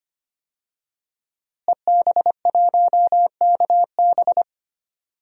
Morse code, heard as a keyed tone, about 25 wpm, 700 Hz. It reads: EB1KB
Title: DXCC-100 Call Signs - 5 Characters - Repeated 3x - 25wpm